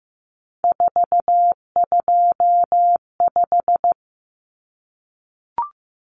Morse code, heard as a keyed tone, about 15 wpm, 700 Hz. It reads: 425
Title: Numbers - 3-Digit 15wpm